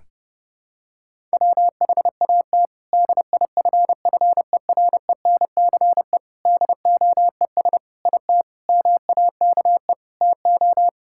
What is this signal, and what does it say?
Morse code, heard as a keyed tone, about 30 wpm, 700 Hz. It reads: WHAT DIFFERENCE DOES IT MAKE TO
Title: Sentences from Top 1000 Words - Review Part 5 - 30wpm